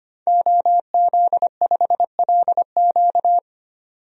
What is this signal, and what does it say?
Morse code, heard as a keyed tone, about 25 wpm, 700 Hz. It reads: OZ5LQ
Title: DXCC-100 Call Signs - 5 Characters - 25wpm